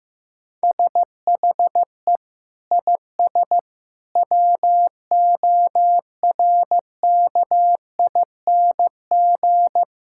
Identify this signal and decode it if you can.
Morse code, heard as a keyed tone, about 15 wpm, 700 Hz. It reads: SHE IS WORKING
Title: Sentences from Top 1000 Words - Review Part 5 - 15wpm